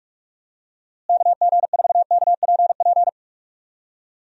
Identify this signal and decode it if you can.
Morse code, heard as a keyed tone, about 42 wpm, 700 Hz. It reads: KG4KPP
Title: USA Call Signs - 6 Characters - Repeated 3x - 45wpm